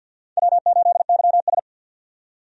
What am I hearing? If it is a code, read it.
Morse code, heard as a keyed tone, about 52 wpm, 700 Hz. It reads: W8YS
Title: USA Call Signs - 4 Characters - 50wpm